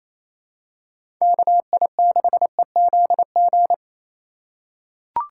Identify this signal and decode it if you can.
Morse code, heard as a keyed tone, about 28 wpm, 700 Hz. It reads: KI6EZG
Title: USA Call Signs - 6 Characters - 3x Character Spacing - 28wpm